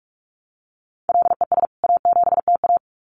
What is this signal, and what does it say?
Morse code, heard as a keyed tone, about 44 wpm, 700 Hz. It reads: LES A7TA
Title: CWT Exchange 45wpm